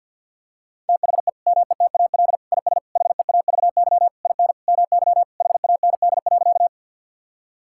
Morse code, heard as a keyed tone, about 50 wpm, 700 Hz. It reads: THE METAL IS HEAVY IN MY HAND.
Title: Sentences from Top 1000 Words - Review Part 6 - 50wpm